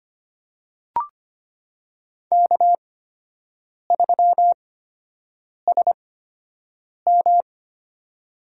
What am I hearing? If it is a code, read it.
Morse code, heard as a keyed tone, about 25 wpm, spaced as for 7 wpm, 700 Hz. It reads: K3SM
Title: USA Call Signs - 4 Characters - 8x Character Spacing - 25wpm